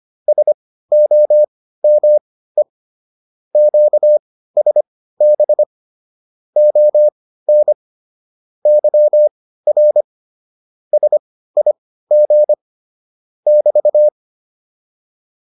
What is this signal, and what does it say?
Morse code, heard as a keyed tone, about 25 wpm, spaced as for 15 wpm, 600 Hz. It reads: SOME QSB ON YR SIG <BT>